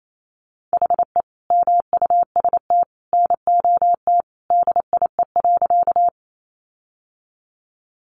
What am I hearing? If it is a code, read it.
Morse code, heard as a keyed tone, about 28 wpm, 700 Hz. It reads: HE MUST NOT DIE.
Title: Sentences from Top 1300 Words 28wpm